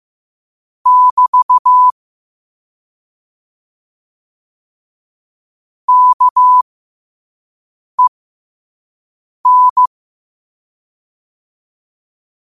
Morse code, heard as a keyed tone, about 15 wpm, spaced as for 5 wpm, 1000 Hz. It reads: <BT> KEN
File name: qotdfiles / 5wpm